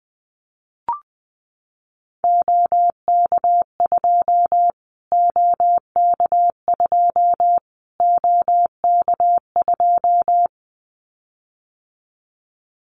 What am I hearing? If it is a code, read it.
Morse code, heard as a keyed tone, about 20 wpm, 700 Hz. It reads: OK2 OK2 OK2
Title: Callsign Prefixes for Top 25 DXCC - Repeated 3x - 20wpm